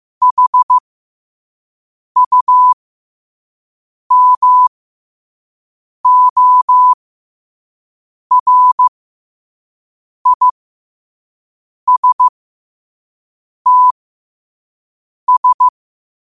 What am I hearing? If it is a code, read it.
Morse code, heard as a keyed tone, about 15 wpm, spaced as for 5 wpm, 1000 Hz. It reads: HUMORISTS
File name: qotdfiles / 5wpm